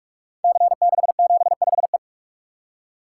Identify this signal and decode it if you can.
Morse code, heard as a keyed tone, about 44 wpm, 700 Hz. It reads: C675E
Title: Code Groups - 5 Letter:Numbers - 45wpm